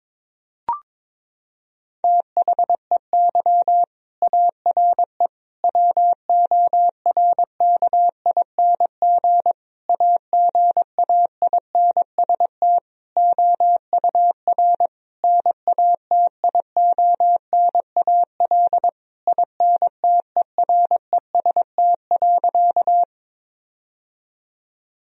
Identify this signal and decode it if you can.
Morse code, heard as a keyed tone, about 22 wpm, 700 Hz. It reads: THEY ARE WORKING AGAINST OUR NATIONAL INTEREST.
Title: Sentences from Top 1000 Words - Review Part 4 - 22wpm